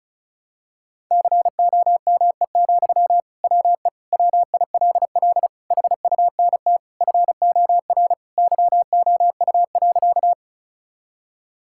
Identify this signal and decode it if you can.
Morse code, heard as a keyed tone, about 34 wpm, 700 Hz. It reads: COME, WE WILL HUNT FOR YOU.
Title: Sentences from Top 1600 Words 35wpm